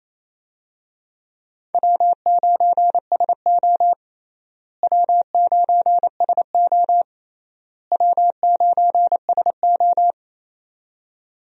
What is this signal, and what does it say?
Morse code, heard as a keyed tone, about 27 wpm, 700 Hz. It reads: W9SO W9SO W9SO
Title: USA Call Signs - 4 Characters - Repeated 3x - 28wpm